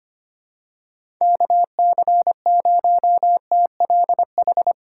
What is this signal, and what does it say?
Morse code, heard as a keyed tone, about 25 wpm, 700 Hz. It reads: KC0TLH
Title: USA Call Signs - 6 Characters - 3x Character Spacing - 25wpm